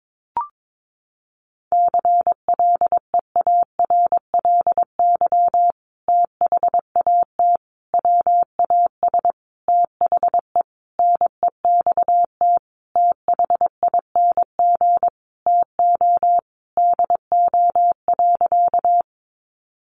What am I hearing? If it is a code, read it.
Morse code, heard as a keyed tone, about 22 wpm, 700 Hz. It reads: CLEARLY THAT WAS THE NEXT THING TO DO.
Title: Sentences from Top 1300 Words 22wpm